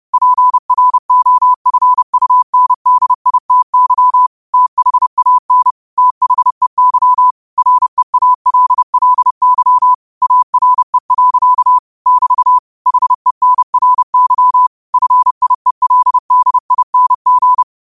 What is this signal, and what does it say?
Morse code, heard as a keyed tone, about 30 wpm, 1000 Hz. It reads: PROFUNDITY THAN THEY REALLY ARE. <BT> HENRY FIELDING
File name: qotdfiles / 30wpm